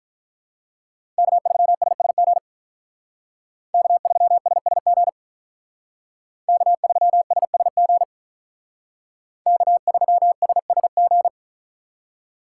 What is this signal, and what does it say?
Morse code, heard as a keyed tone, about 43 wpm, 700 Hz. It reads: K3SSG K3SSG K3SSG K3SSG
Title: Speed Racing - USA 1x3 Call Signs 35wpm